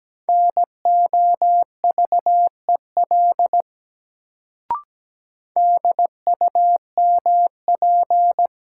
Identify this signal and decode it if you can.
Morse code, heard as a keyed tone, about 17 wpm, 700 Hz. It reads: NOVEL DUMP
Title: Sentences from Top 1300 Words - New Word Drill - 17wpm